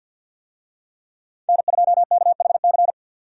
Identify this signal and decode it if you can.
Morse code, heard as a keyed tone, about 46 wpm, 700 Hz. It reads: N2KHC
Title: USA Call Signs - 5 Characters - 50wpm